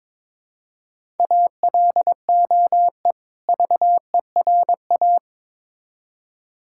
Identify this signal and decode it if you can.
Morse code, heard as a keyed tone, about 22 wpm, 700 Hz. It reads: ALOE VERA
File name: Difficult Words with Hints - 22wpm